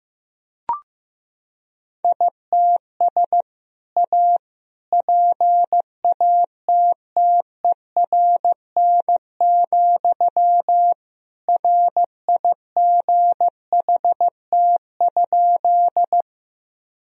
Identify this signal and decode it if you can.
Morse code, heard as a keyed tone, about 16 wpm, 700 Hz. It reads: ITS A PATTERN, RIGHT?
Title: Sentences from Top 1000 Words - Review Part 2 - 15wpm